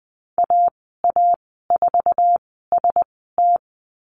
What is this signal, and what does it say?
Morse code, heard as a keyed tone, about 20 wpm, spaced as for 15 wpm, 700 Hz. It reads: AA4ST
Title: 062 Lesson 11g - Letter h - Call Signs 20wpm